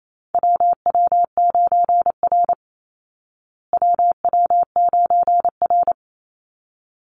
Morse code, heard as a keyed tone, about 28 wpm, 700 Hz. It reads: WW9R WW9R
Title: Call Signs Repeated 3x - 28wpm